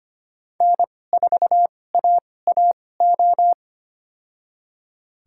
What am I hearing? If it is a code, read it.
Morse code, heard as a keyed tone, about 25 wpm, spaced as for 18 wpm, 700 Hz. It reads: N4AAO
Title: USA Call Signs - 2x Character Spacing - 25wpm